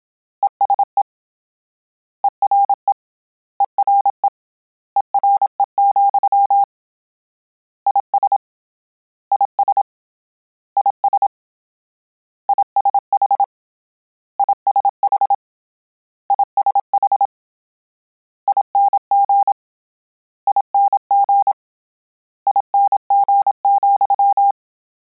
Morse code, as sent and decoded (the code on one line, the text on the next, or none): . ... / . .-. / . .-. / . .-. --..-- / .. ... / .. ... / .. ... / .. ... .... / .. ... .... / .. ... .... / .. -. --. / .. -. --. / .. -. --. --..--
ESE ERE ERE ERE, IS IS IS ISH ISH ISH ING ING ING,